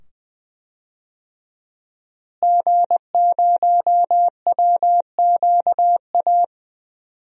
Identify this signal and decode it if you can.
Morse code, heard as a keyed tone, about 20 wpm, 700 Hz. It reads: G0WQA
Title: DXCC-100 Call Signs - 5 Characters - Repeated 2x - 20wpm